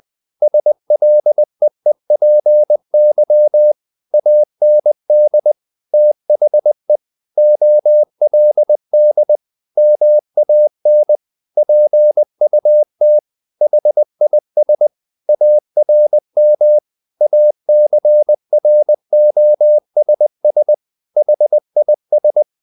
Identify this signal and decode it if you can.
Morse code, heard as a keyed tone, about 20 wpm, 600 Hz. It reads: SLEEPY AND THE OLD MAN PUT HIS ARM ACROSS HIS
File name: the old man and the sea / the old man and the sea part 1 20wpm